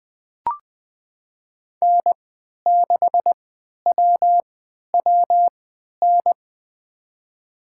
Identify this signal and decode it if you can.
Morse code, heard as a keyed tone, about 20 wpm, spaced as for 11 wpm, 700 Hz. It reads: N6WWN
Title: USA Call Signs - 5 Characters - 3x Character Spacing - 20wpm